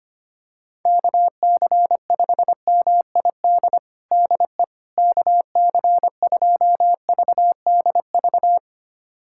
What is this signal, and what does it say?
Morse code, heard as a keyed tone, about 25 wpm, 700 Hz. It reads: KC5MID DE KC2VDV
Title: USA Callsign DE Callsign - 6 Characters - 25wpm